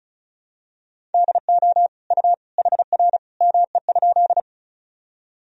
Morse code, heard as a keyed tone, about 34 wpm, 700 Hz. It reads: DO U HR ME?
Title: Sentences from Top 300 Words and Abbreviations 35wpm